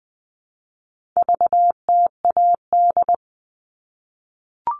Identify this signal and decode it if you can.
Morse code, heard as a keyed tone, about 20 wpm, 700 Hz. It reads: VTAD
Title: All Cut Numbers - 4-Digit 20wpm